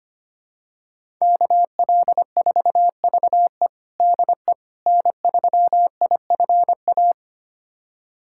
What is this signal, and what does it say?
Morse code, heard as a keyed tone, about 25 wpm, 700 Hz. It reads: KL4VE DE N3IFA
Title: USA Callsign DE Callsign - 5 Characters - 25wpm